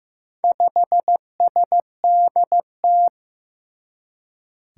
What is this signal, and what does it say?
Morse code, heard as a keyed tone, about 15 wpm, 700 Hz. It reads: 5SDT
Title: Four Challenging Letter-Number Combinations 15wpm